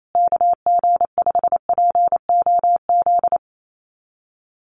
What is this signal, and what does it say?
Morse code, heard as a keyed tone, about 28 wpm, 700 Hz. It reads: KG5POZ
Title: USA Call Signs Repeated 2x 28wpm